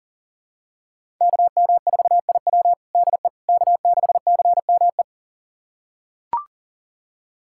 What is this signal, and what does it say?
Morse code, heard as a keyed tone, about 40 wpm, 700 Hz. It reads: KM4IW DE K6CME
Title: USA Callsign DE Callsign - 5 Characters - 40wpm